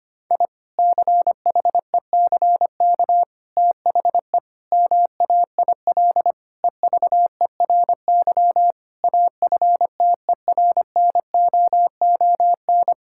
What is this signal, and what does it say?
Morse code, heard as a keyed tone, about 25 wpm, 700 Hz. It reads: I CHECK THE MAIL EVERY AFTERNOON
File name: practice / 25wpm / Difficult Sentences with Hints - 25wpm